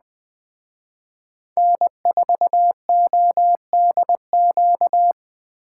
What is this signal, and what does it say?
Morse code, heard as a keyed tone, about 20 wpm, 700 Hz. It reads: N4ODQ
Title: Speed Racing - USA 1x3 Call Signs 17wpm